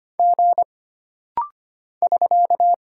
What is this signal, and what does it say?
Morse code, heard as a keyed tone, about 25 wpm, 700 Hz. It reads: G <SK>
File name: Single Character (Letter:Number:Punctuation:Prosign) - Rapid-Fire - 25wpm